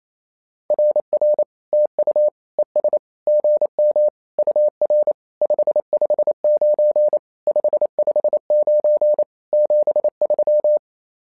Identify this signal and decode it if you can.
Morse code, heard as a keyed tone, about 28 wpm, 600 Hz. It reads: RR TU ES GM UR 559 559 73
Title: SOTA QSO Practice 28wpm